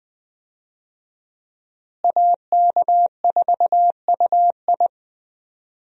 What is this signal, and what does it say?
Morse code, heard as a keyed tone, about 20 wpm, 700 Hz. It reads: AK4UI
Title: USA Call Signs - 5 Characters - Repeated 3x - 20wpm